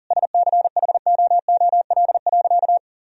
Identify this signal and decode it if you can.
Morse code, heard as a keyed tone, about 40 wpm, 700 Hz. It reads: SCHOOL.
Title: Sentences from Top 1000 Words - Review Part 1 - 40wpm